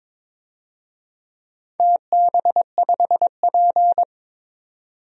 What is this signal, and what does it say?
Morse code, heard as a keyed tone, about 22 wpm, 700 Hz. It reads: TB5P
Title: DXCC-100 Call Signs - 4 Characters - 22wpm